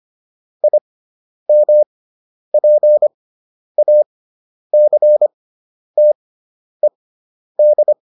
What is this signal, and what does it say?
Morse code, heard as a keyed tone, about 25 wpm, spaced as for 10 wpm, 600 Hz. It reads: IMPACTED